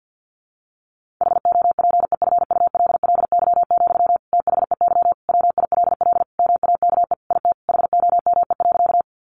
Morse code, heard as a keyed tone, about 50 wpm, 700 Hz. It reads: HOPEFULLY, THEY WILL MAKE IT HOME.
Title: Sentences from Top 1500 Words 50wpm